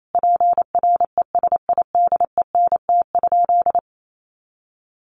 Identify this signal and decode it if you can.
Morse code, heard as a keyed tone, about 28 wpm, 700 Hz. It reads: PRESIDENT?
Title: Sentences from Top 1300 Words 28wpm